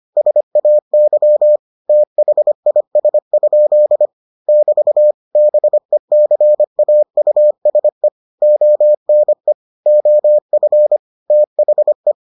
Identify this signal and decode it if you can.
Morse code, heard as a keyed tone, about 25 wpm, 600 Hz. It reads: SAY THIS? <BT> BECAUSE ONE OF THE